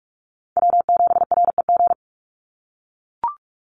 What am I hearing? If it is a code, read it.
Morse code, heard as a keyed tone, about 44 wpm, 700 Hz. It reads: W7REG
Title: USA Call Signs 45wpm